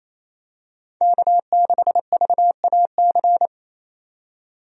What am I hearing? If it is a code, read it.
Morse code, heard as a keyed tone, about 28 wpm, 700 Hz. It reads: K6VAC
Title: USA Call Signs - 5 Characters - 28wpm